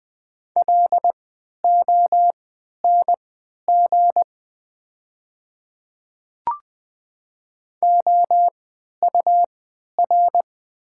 Morse code, heard as a keyed tone, about 20 wpm, spaced as for 11 wpm, 690 Hz. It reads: LONG OUR
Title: 3x Character Spacing - Top 200 Words - 20wpm